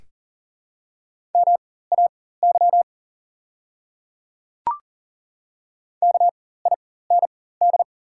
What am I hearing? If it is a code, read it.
Morse code, heard as a keyed tone, about 40 wpm, spaced as for 19 wpm, 700 Hz. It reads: MAY KIND
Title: 4x Character Spacing - Top 200 Words - 40wpm